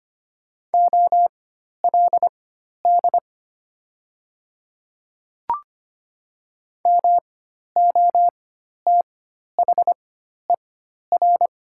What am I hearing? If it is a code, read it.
Morse code, heard as a keyed tone, about 25 wpm, spaced as for 12 wpm, 700 Hz. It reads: OLD MOTHER
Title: 4x Character Spacing - Top 200 Words - 25wpm